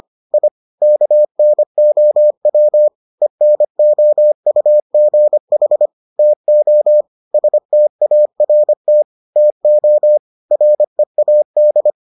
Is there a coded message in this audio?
Yes: Morse code from 0.3 to 11.9 s, heard as a keyed tone, about 25 wpm, 600 Hz. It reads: I KNOW ENOUGH TO START TO READ